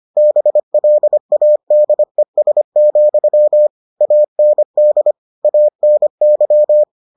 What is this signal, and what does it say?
Morse code, heard as a keyed tone, about 25 wpm, 600 Hz. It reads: BLADES, AND ANY